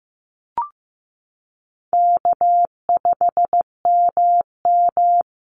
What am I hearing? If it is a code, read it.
Morse code, heard as a keyed tone, about 15 wpm, 700 Hz. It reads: K5MM
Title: Call Signs Repeated 2x - 15wpm